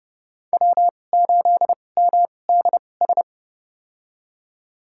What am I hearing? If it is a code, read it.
Morse code, heard as a keyed tone, about 32 wpm, spaced as for 22 wpm, 700 Hz. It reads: W8MDS
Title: USA Call Signs - 2x Character Spacing - 30wpm